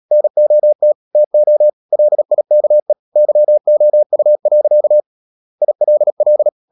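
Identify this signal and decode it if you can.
Morse code, heard as a keyed tone, about 37 wpm, 600 Hz. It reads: NOT TO LIKE YOU. ILL
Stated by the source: Morse practice at 36 wpm, text of Ken Follett's novel Never, from CW med Gnister.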